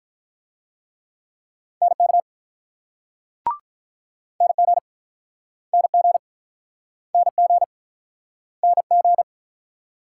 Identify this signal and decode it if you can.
Morse code, heard as a keyed tone, about 49 wpm, 700 Hz. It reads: NK NG NG NG NG